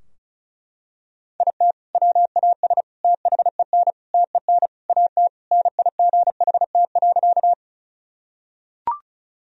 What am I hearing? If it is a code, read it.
Morse code, heard as a keyed tone, about 35 wpm, 700 Hz. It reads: IT WAS THEN TEN AT NIGHT.
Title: Sentences from Top 1000 Words - Review Part 2 - 35wpm